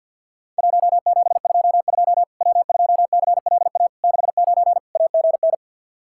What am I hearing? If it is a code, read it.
Morse code, heard as a keyed tone, about 50 wpm, 700 Hz. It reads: 1722 W2CLA 69 AGN
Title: Worked All Europe DX Contest - QTC Exchange with Repeats - 50wpm